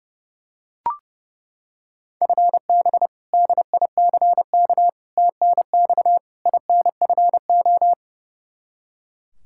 Morse code, heard as a keyed tone, about 30 wpm, 700 Hz. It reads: FB DICK TNX INFO
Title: Practice QSO - Exchange 3 - More Info - 30wpm